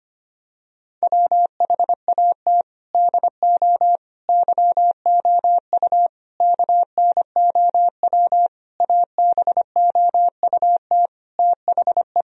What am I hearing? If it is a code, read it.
Morse code, heard as a keyed tone, about 25 wpm, 700 Hz. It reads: WHAT DO YOU KNOW ABOUT THE
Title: Sentences from Top 1000 Words - Review Part 4 - 25wpm